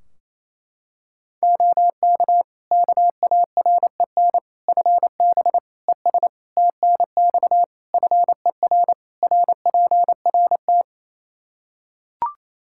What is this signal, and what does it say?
Morse code, heard as a keyed tone, about 27 wpm, 700 Hz. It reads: OK KAREN FB ES TNX FER RPRT
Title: Practice QSO - Exchange 2 - Info - 28wpm